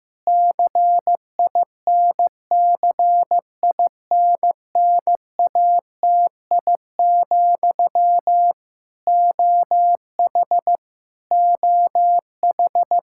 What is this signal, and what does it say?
Morse code, heard as a keyed tone, about 16 wpm, 700 Hz. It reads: CINCINNATI, OH OH
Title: K1USN SST QSO Practice 15wpm